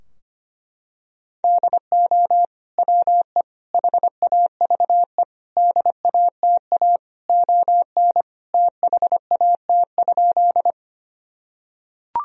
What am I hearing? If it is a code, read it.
Morse code, heard as a keyed tone, about 25 wpm, 700 Hz. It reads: DO WE HAVE DATA ON THAT?
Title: Sentences from Top 1000 Words - Review Part 3 - 25wpm